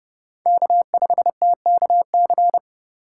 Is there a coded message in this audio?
Yes: Morse code from 0.5 to 2.6 s, heard as a keyed tone, about 29 wpm, 700 Hz. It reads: K5TKC